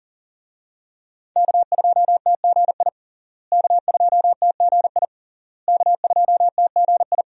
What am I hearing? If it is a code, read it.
Morse code, heard as a keyed tone, about 40 wpm, 700 Hz. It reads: K2TGI K2TGI K2TGI
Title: USA Call Signs - 5 Characters - Repeated 3x - 40wpm